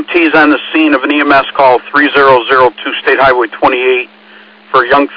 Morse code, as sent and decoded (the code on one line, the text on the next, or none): none